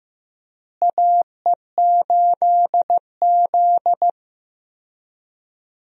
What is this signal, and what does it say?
Morse code, heard as a keyed tone, about 16 wpm, 700 Hz. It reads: AE8Z
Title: USA Call Signs - 4 Characters - Repeated 3x - 15wpm